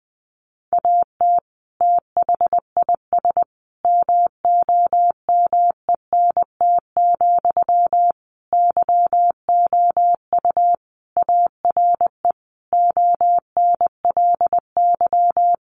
Morse code, heard as a keyed tone, about 20 wpm, 700 Hz. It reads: AT THIS MOMENT, YOU ARE ONLY
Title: Sentences from Top 1000 Words 20wpm